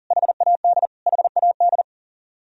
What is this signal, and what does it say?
Morse code, heard as a keyed tone, about 40 wpm, 700 Hz. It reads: HAD HAD
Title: Top 100 Words Repeated 40wpm